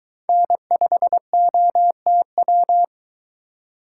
Morse code, heard as a keyed tone, about 21 wpm, 700 Hz. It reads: N5OTW